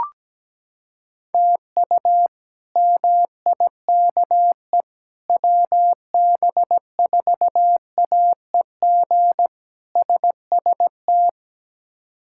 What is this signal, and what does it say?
Morse code, heard as a keyed tone, about 17 wpm, 700 Hz. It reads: TU MIKE WB4AEG SST
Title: K1USN SST QSO Practice - 17wpm